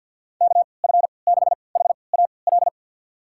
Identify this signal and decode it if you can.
Morse code, heard as a keyed tone, about 49 wpm, spaced as for 28 wpm, 700 Hz. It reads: KF6HAL
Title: USA Call Signs - 6 Characters - 3x Character Spacing - 50wpm